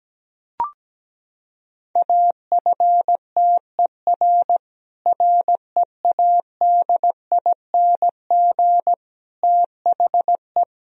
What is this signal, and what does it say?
Morse code, heard as a keyed tone, about 17 wpm, 700 Hz. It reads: AFTER READING THE
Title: Sentences from Top 1000 Words - Review Part 3 - 17wpm